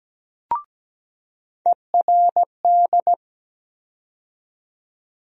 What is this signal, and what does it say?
Morse code, heard as a keyed tone, about 17 wpm, 700 Hz. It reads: ERD
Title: USA Call Sign Postfixes - 3 Letters 17wpm